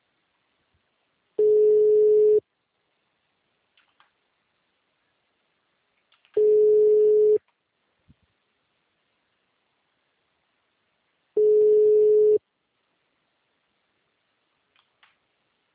Ringback tone (Europe, 425 Hz).